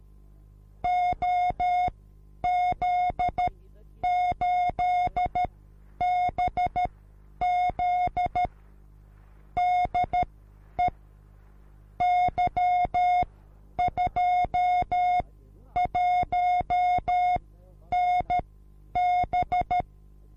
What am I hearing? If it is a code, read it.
Morse code, heard as a keyed tone, about 13 wpm, spaced as for 9 wpm, 720 Hz. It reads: OZ8BZ DE Y21NB
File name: repos / Morsekurs / DJ4UF Morsen24